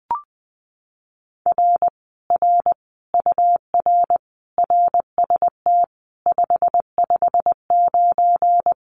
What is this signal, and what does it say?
Morse code, heard as a keyed tone, about 20 wpm, 700 Hz. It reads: R R UR RST 559
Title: Common QSO Phrases Encore 20wpm